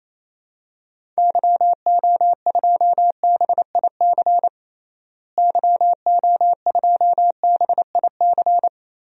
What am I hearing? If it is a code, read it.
Morse code, heard as a keyed tone, about 28 wpm, 700 Hz. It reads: YO2BIC YO2BIC
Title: DXCC-100 Call Signs - 6 Characters - Repeated 3x - 28wpm